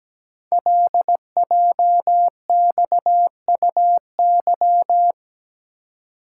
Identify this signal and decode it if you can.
Morse code, heard as a keyed tone, about 17 wpm, 700 Hz. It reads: LJXUY
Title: Code Groups - 5 Letter:Numbers - 17wpm